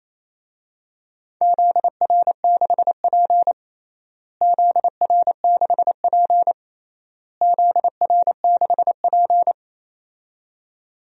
Morse code, heard as a keyed tone, about 28 wpm, 700 Hz. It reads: ZR6P ZR6P ZR6P
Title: DXCC-100 Call Signs - 4 Characters - Repeated 3x - 28wpm